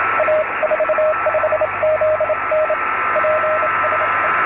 Morse code, heard as a keyed tone, about 27 wpm, 620 Hz, with radio noise. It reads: A45ZN PSE